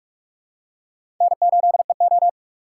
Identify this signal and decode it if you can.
Morse code, heard as a keyed tone, about 44 wpm, 700 Hz. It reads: N8EO
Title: USA Call Signs - 4 Characters - 6x Character Spacing - 45wpm